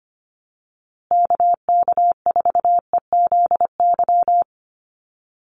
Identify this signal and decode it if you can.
Morse code, heard as a keyed tone, about 25 wpm, 700 Hz. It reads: KK4EZY
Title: USA Call Signs Repeated 2x 25wpm